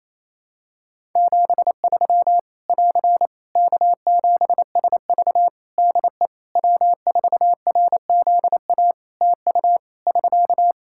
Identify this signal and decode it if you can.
Morse code, heard as a keyed tone, about 28 wpm, 700 Hz. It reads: 73 <AR> K7SV DE W4RZA TU <SK>
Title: Practice QSO - Exchange 4 - Ending - 28wpm